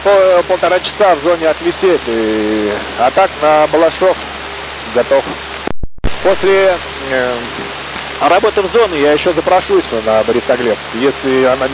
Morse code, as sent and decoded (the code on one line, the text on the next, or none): none